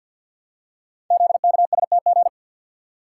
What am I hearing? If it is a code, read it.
Morse code, heard as a keyed tone, about 47 wpm, 700 Hz. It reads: ZKSTG
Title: Code Groups - 5 Letters - 50wpm